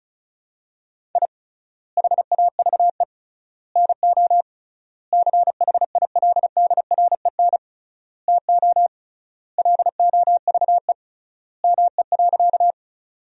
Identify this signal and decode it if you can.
Morse code, heard as a keyed tone, about 34 wpm, 700 Hz. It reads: I HAVE NO CHILDREN TO LOVE ME.
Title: 3x Word-Spacing - Sentences from Top 400 Words - 35wpm